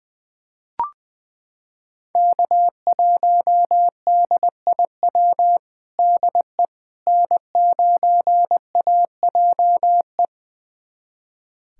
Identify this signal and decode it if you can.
Morse code, heard as a keyed tone, about 20 wpm, 700 Hz. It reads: K1DIW DE N9AJE
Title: USA Callsign DE Callsign - 5 Characters - 20wpm